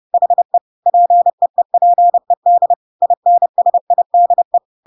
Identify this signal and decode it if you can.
Morse code, heard as a keyed tone, about 30 wpm, 700 Hz. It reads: HE PEEPED INSIDE
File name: Pinocchio / Pinocchio ch1 30 30 ik1hge